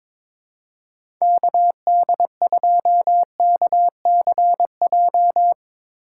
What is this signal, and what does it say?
Morse code, heard as a keyed tone, about 22 wpm, 700 Hz. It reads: KD2KCJ
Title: USA Call Signs - 4x Character Spacing - 22wpm